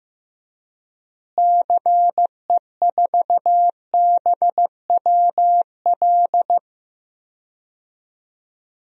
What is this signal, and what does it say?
Morse code, heard as a keyed tone, about 15 wpm, 700 Hz. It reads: CE4BWL
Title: DXCC-100 Call Signs - 6 Characters - Repeated 2x - 15wpm